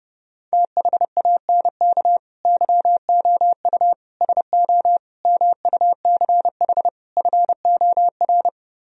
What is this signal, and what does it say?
Morse code, heard as a keyed tone, about 30 wpm, 700 Hz. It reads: THANK YOU SO MUCH FOR